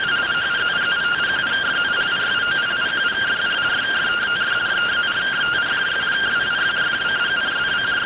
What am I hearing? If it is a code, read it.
Two-tone teleprinter signal, about 170 Hz shift.